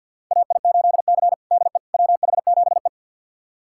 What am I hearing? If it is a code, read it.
Morse code, heard as a keyed tone, about 51 wpm, 700 Hz. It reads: AI8C DE WH7E
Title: USA Callsign DE Callsign - 4 Characters - 50wpm